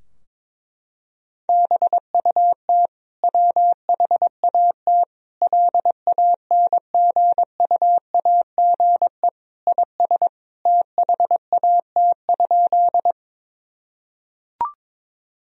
Morse code, heard as a keyed tone, about 22 wpm, 700 Hz. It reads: BUT WHAT LANGUAGE IS THAT?
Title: Sentences from Top 1000 Words - Review Part 3 - 22wpm